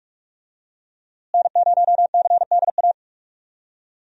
Morse code, heard as a keyed tone, about 41 wpm, 700 Hz. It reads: N0CDA
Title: USA Call Signs - 5 Characters - Repeated 3x - 45wpm